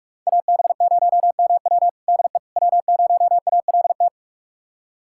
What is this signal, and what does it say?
Morse code, heard as a keyed tone, about 43 wpm, 700 Hz. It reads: AB0MW DE W0ALT
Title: USA Callsign DE Callsign - 5 Characters - 45wpm